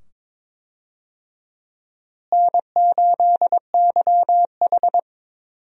Morse code, heard as a keyed tone, about 21 wpm, 700 Hz. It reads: N8YH
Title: USA Call Signs - 4 Characters - Repeated 2x - 22wpm